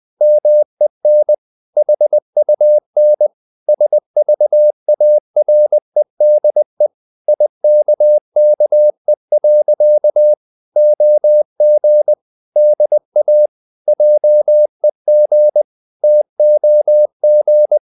Morse code, heard as a keyed tone, about 20 wpm, 600 Hz. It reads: MEN HUN SVAREDE IKKE. OG DA JEG TOG
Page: Ferie 20 wpm